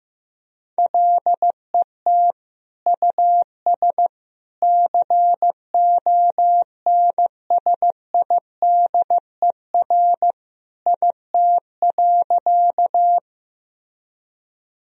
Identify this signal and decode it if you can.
Morse code, heard as a keyed tone, about 15 wpm, 700 Hz. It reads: LET US CONSIDER IT.
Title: Sentences from Top 1000 Words - Review Part 4 - 15wpm